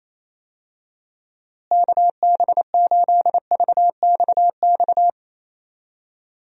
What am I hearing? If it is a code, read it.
Morse code, heard as a keyed tone, about 28 wpm, 700 Hz. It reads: KB8VXX
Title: USA Call Signs - 6 Characters - Repeated 2x - 28wpm